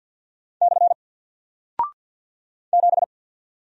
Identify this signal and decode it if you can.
Morse code, heard as a keyed tone, about 48 wpm, 700 Hz. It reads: / 7
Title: Single Character (Letter:Number:Punctuation:Prosign) 50wpm